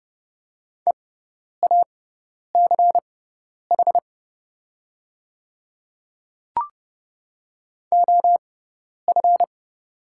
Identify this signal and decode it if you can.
Morse code, heard as a keyed tone, about 29 wpm, spaced as for 10 wpm, 700 Hz. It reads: EACH OF
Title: 6x Character Spacing - Top 100 Words - 30wpm